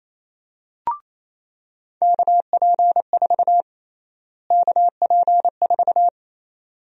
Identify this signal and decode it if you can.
Morse code, heard as a keyed tone, about 28 wpm, 700 Hz. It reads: KP4 KP4
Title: USA Call Sign Prefixes - Repeated 2x - 28wpm